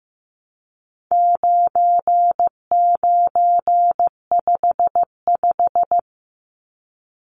Morse code, heard as a keyed tone, about 15 wpm, 700 Hz. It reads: 9955
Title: Numbers - 4-Digit 15wpm